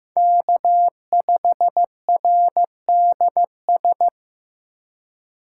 Morse code, heard as a keyed tone, about 18 wpm, 700 Hz. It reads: K5RDS
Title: Speed Racing - USA 1x3 Call Signs 15wpm